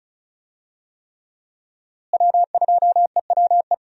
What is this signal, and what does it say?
Morse code, heard as a keyed tone, about 36 wpm, 700 Hz. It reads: W2EWE
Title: USA Call Signs - 5 Characters - Repeated 3x - 35wpm